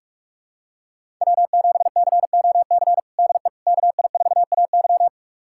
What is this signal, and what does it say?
Morse code, heard as a keyed tone, about 44 wpm, 700 Hz. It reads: W7COC DE KI4AY
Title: USA Callsign DE Callsign - 5 Characters - 45wpm